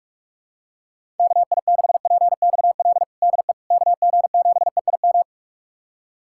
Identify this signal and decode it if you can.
Morse code, heard as a keyed tone, about 41 wpm, 700 Hz. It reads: KI6PXL DE KG7EIM
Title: USA Callsign DE Callsign 45wpm